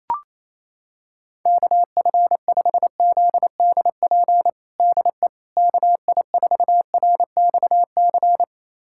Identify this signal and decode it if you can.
Morse code, heard as a keyed tone, about 28 wpm, 700 Hz. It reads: KF5ZDP DE KI4RXC
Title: USA Callsign DE Callsign 28wpm